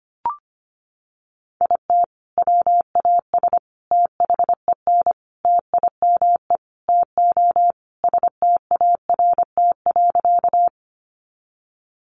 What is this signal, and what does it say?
Morse code, heard as a keyed tone, about 25 wpm, 700 Hz. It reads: IT WAS THEN TIME TO START.
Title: Sentences from Top 300 Words 25wpm